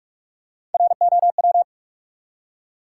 Morse code, heard as a keyed tone, about 47 wpm, 700 Hz. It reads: ROW
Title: Three Letter Words - 45wpm